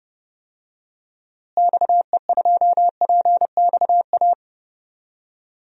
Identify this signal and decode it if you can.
Morse code, heard as a keyed tone, about 30 wpm, 700 Hz. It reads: XE2PXA
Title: DXCC-100 Call Signs - 6 Characters - Repeated 3x - 30wpm